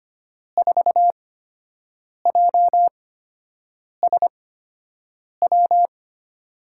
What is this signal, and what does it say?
Morse code, heard as a keyed tone, about 25 wpm, spaced as for 7 wpm, 700 Hz. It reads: 4JSW
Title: USA Call Signs - 8x Character Spacing - 25wpm